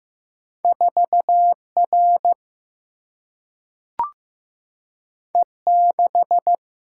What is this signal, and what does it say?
Morse code, heard as a keyed tone, about 15 wpm, 700 Hz. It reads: 4 R E 6
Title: Single-Letter Single-Number 15wpm